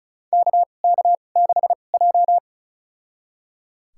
Morse code, heard as a keyed tone, about 36 wpm, spaced as for 25 wpm, 700 Hz. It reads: KK6J
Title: USA Call Signs - 4 Characters - 2x Character Spacing - 35wpm